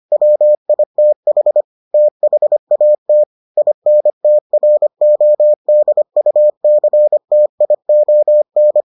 Morse code, heard as a keyed tone, about 25 wpm, 600 Hz. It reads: WITH THAT INTRODUCTION